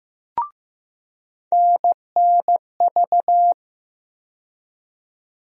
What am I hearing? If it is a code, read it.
Morse code, heard as a keyed tone, about 15 wpm, 700 Hz. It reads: NNV
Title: USA Call Sign Postfixes - 3 Letters 15wpm